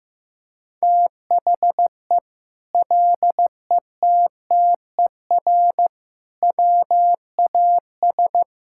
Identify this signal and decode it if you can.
Morse code, heard as a keyed tone, about 16 wpm, 700 Hz. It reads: THE LETTER WAS
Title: Sentences from Top 1000 Words - Review Part 5 - 15wpm